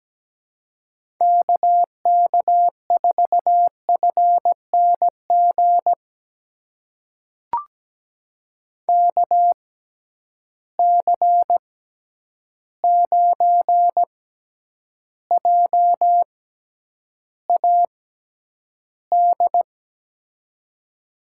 Morse code, heard as a keyed tone, about 17 wpm, spaced as for 9 wpm, 700 Hz. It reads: KK4FNG KC9JAD
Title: USA Call Signs - 6 Characters - 6x Character Spacing - 17wpm